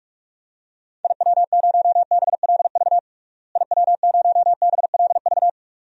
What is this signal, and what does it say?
Morse code, heard as a keyed tone, about 46 wpm, 700 Hz. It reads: IW0BLV IW0BLV
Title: DXCC-100 Call Signs - 6 Characters - Repeated 2x - 45wpm